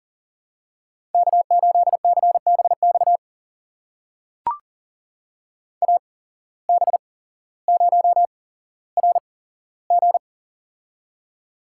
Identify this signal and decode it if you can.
Morse code, heard as a keyed tone, about 42 wpm, spaced as for 17 wpm, 700 Hz. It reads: K8CBX AB0RG
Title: USA Call Signs - 5 Characters - 8x Character Spacing - 40wpm